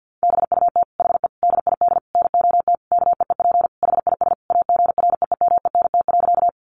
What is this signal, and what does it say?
Morse code, heard as a keyed tone, about 50 wpm, 700 Hz. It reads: BUT HE DID NOT KEEP HIS AGREEMENT.
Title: Sentences from Top 1500 Words 50wpm